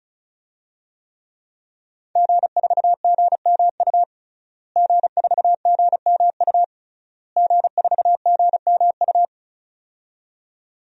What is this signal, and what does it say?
Morse code, heard as a keyed tone, about 36 wpm, 700 Hz. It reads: G4GMU G4GMU G4GMU
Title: DXCC-100 Call Signs - 5 Characters - Repeated 3x - 35wpm